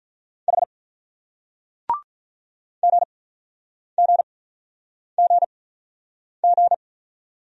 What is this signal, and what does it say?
Morse code, heard as a keyed tone, about 45 wpm, spaced as for 9 wpm, 700 Hz. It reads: H GGGG